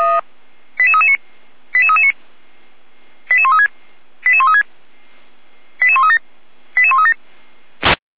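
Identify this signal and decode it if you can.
Five-tone selective call (ZVEI), heard as quick runs of five tones, tones of about 70 ms. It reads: 88339 88339 88136 88136 88137 88137
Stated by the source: ZVEI five-tone call 88190 received on SDR.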